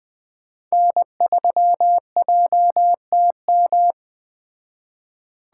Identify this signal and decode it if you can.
Morse code, heard as a keyed tone, about 20 wpm, 700 Hz. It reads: N3JTM